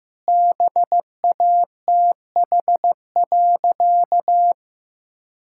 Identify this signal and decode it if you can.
Morse code, heard as a keyed tone, about 15 wpm, 700 Hz. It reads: BATH.
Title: Sentences from Top 1600 Words 15wpm